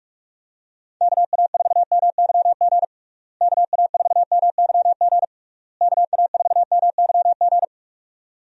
Morse code, heard as a keyed tone, about 44 wpm, 700 Hz. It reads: KA4MYG KA4MYG KA4MYG
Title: USA Call Signs - 6 Characters - Repeated 3x - 45wpm